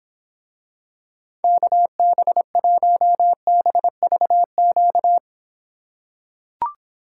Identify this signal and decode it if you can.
Morse code, heard as a keyed tone, about 25 wpm, 700 Hz. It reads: KB1BVQ